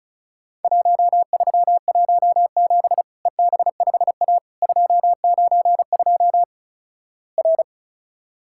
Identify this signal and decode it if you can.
Morse code, heard as a keyed tone, about 36 wpm, 700 Hz. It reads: 1317 EB5A 292 R
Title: Worked All Europe DX Contest - QTC Exchange - 35wpm